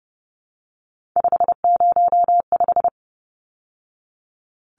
Morse code, heard as a keyed tone, about 30 wpm, 700 Hz. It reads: 505
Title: Numbers - 3-Digit 30wpm